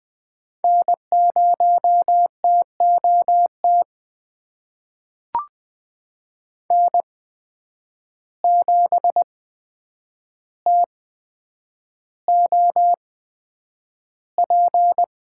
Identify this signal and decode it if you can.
Morse code, heard as a keyed tone, about 20 wpm, spaced as for 9 wpm, 700 Hz. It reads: N0TOT N7TOP
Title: USA Call Signs - 5 Characters - 8x Character Spacing - 20wpm